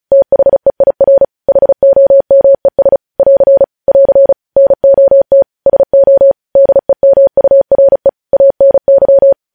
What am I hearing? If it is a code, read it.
Morse code, heard as a keyed tone, about 36 wpm, 550 Hz. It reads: THEIR HOMES <AR> <AR> NOT SO DEMURE ANY
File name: podcast / News headlines-1x-35x35